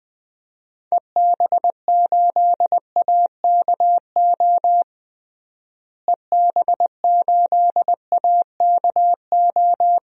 Morse code, heard as a keyed tone, about 21 wpm, 700 Hz. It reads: EB8AKO EB8AKO
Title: DXCC-100 Call Signs - 6 Characters - Repeated 3x - 20wpm